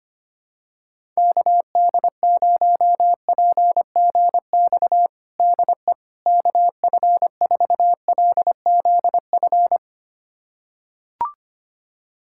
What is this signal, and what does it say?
Morse code, heard as a keyed tone, about 25 wpm, 700 Hz. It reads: KD0PGX DE KF4LZF
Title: USA Callsign DE Callsign 25wpm